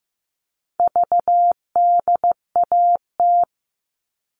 Morse code, heard as a keyed tone, about 15 wpm, 700 Hz. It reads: VDAT
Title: All Cut Numbers - 4-Digit 15wpm